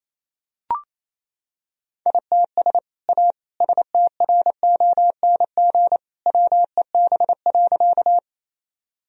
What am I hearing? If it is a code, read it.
Morse code, heard as a keyed tone, about 28 wpm, 700 Hz. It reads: ITS A STRONG WEB.
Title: Sentences from Top 1000 Words - Review Part 4 - 28wpm